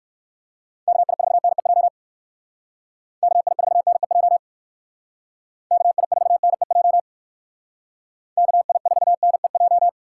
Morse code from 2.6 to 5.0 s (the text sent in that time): KI4NEJ